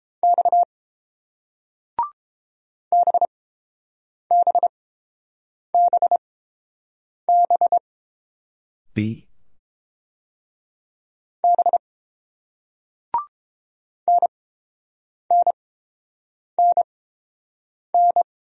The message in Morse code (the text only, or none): X BBBBB NNNN